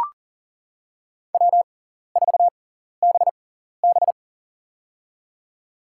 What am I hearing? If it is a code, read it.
Morse code, heard as a keyed tone, about 40 wpm, spaced as for 14 wpm, 700 Hz. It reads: W4BB
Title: USA Call Signs - 6x Character Spacing - 40wpm